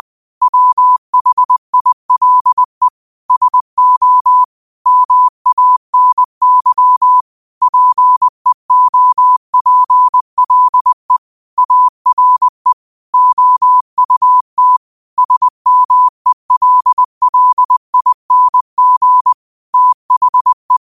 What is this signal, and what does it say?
Morse code, heard as a keyed tone, about 20 wpm, 1000 Hz. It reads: WHILE SO MANY PEOPLE ARE OUT SMELLING THE
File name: qotdfiles / 20wpm